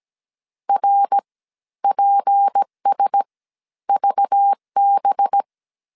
Morse code, heard as a keyed tone, about 17 wpm, spaced as for 12 wpm, 800 Hz. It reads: RPSVB
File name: CW / DET / 1 6 1500C S A 16wpm 2635